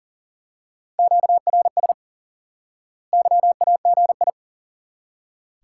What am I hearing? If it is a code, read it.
Morse code, heard as a keyed tone, about 40 wpm, 700 Hz. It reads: QRS YAGI
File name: IQE Course 3 - Lesson 89 - YAGI - Two Elements - 40wpm